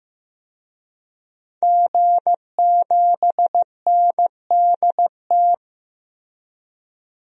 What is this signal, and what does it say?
Morse code, heard as a keyed tone, about 15 wpm, 700 Hz. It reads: G7NDT
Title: DXCC-100 Call Signs - 5 Characters - Repeated 2x - 15wpm